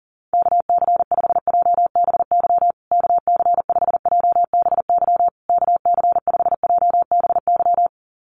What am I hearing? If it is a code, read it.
Morse code, heard as a keyed tone, about 40 wpm, 700 Hz. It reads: KC5JBY KC5JBY KC5JBY
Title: USA Call Signs Repeated 3x 40wpm